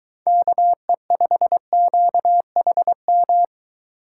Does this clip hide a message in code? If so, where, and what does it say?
Yes: Morse code from 0.3 to 3.4 s, heard as a keyed tone, about 21 wpm, 700 Hz. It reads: KE5QHM